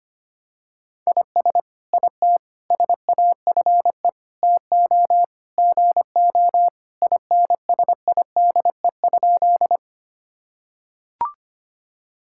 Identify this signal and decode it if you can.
Morse code, heard as a keyed tone, about 25 wpm, 700 Hz. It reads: IS IT SAFE TO GO INSIDE?
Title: Sentences from Top 1000 Words - Review Part 6 - 25wpm